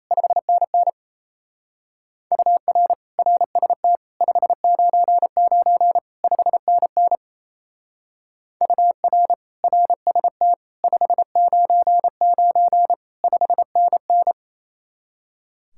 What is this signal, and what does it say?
Morse code, heard as a keyed tone, about 32 wpm, 700 Hz. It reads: 5NN UR RST 599 5NN UR RST 599 5NN
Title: Speed-Racing Common QSO Phrases - 28wpm